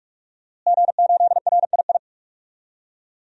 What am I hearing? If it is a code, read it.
Morse code, heard as a keyed tone, about 45 wpm, 700 Hz. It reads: G8RII
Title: DXCC-100 Call Signs - 5 Characters - Repeated 3x - 45wpm